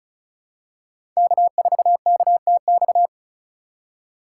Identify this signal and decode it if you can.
Morse code, heard as a keyed tone, about 36 wpm, 700 Hz. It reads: K4KTX
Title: USA Call Signs - 5 Characters - 4x Character Spacing - 35wpm